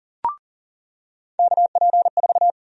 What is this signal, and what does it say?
Morse code, heard as a keyed tone, about 38 wpm, 700 Hz. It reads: KP4
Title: USA Call Sign Prefixes - 40wpm